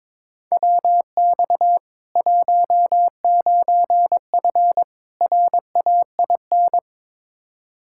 Morse code, heard as a keyed tone, about 22 wpm, 700 Hz. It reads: WX 19F RAIN
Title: K1USN SST QSO Practice 22wpm